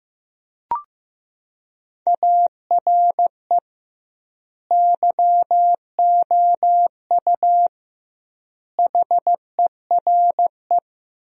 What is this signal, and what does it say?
Morse code, heard as a keyed tone, about 16 wpm, 700 Hz. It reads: ARE YOU HERE
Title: 2x Word-Spacing - Sentences from Top 400 Words - 15wpm